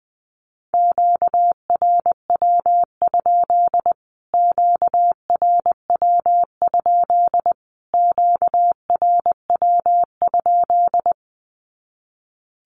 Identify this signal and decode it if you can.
Morse code, heard as a keyed tone, about 20 wpm, 700 Hz. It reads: QRW? QRW? QRW?
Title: Q-Codes Repeated 3x 20wpm